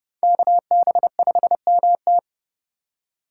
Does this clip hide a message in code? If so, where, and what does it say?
Yes: Morse code from 0.2 to 2.2 s, heard as a keyed tone, about 29 wpm, 700 Hz. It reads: KB5MT